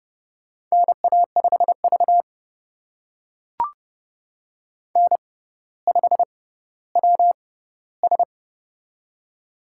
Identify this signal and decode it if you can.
Morse code, heard as a keyed tone, about 30 wpm, spaced as for 15 wpm, 700 Hz. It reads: NA5V N5WS
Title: USA Call Signs - 4 Characters - 6x Character Spacing - 30wpm